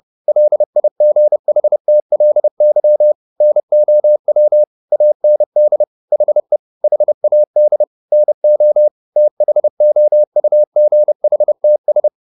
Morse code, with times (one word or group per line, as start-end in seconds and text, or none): 0.3-3.1 LIGHTLY
3.4-4.6 NOW
4.9-5.8 AND
6.1-6.6 HE
6.8-7.8 HAD
8.1-8.9 NO
9.2-12.1 THOUGHTS